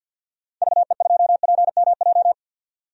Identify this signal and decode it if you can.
Morse code, heard as a keyed tone, about 48 wpm, 700 Hz. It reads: VE2PMJ
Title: DXCC-100 Call Signs - 6 Characters - 50wpm